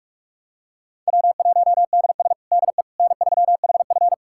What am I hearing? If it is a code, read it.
Morse code, heard as a keyed tone, about 41 wpm, 700 Hz. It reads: W1DS DE N3HF
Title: USA Callsign DE Callsign - 4 Characters - 45wpm